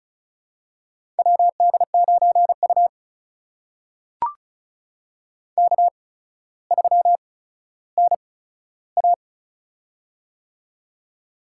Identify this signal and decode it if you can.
Morse code, heard as a keyed tone, about 36 wpm, spaced as for 15 wpm, 700 Hz. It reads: WD9U K3NA
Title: USA Call Signs - 4 Characters - 8x Character Spacing - 35wpm